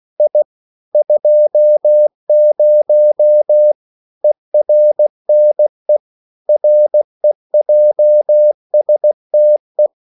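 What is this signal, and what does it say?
Morse code, heard as a keyed tone, about 16 wpm, 600 Hz. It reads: I 20 ERNE REJSTE
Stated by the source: Slow (QRS) Morse code at 16 wpm, from CW med Gnister.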